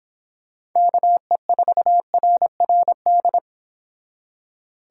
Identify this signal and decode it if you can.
Morse code, heard as a keyed tone, about 26 wpm, 700 Hz. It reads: KE4RRD